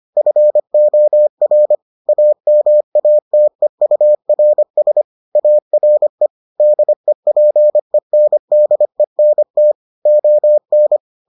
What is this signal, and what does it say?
Morse code, heard as a keyed tone, about 25 wpm, 600 Hz. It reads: FOR AMATEURS ARE DEPENDENT ON